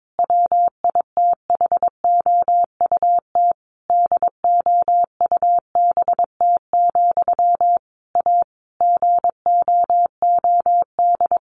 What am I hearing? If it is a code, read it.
Morse code, heard as a keyed tone, about 22 wpm, 700 Hz. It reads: WITHOUT DOUBT, A GOOD
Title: Sentences from Top 1500 Words 22wpm